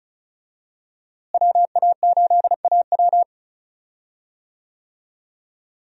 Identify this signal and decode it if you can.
Morse code, heard as a keyed tone, about 34 wpm, 700 Hz. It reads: WA8AW
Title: USA Call Signs - 5 Characters - Repeated 2x - 35wpm